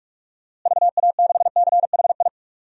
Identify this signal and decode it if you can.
Morse code, heard as a keyed tone, about 42 wpm, 700 Hz. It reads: VA6CHI
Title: DXCC-100 Call Signs - 6 Characters - Repeated 2x - 45wpm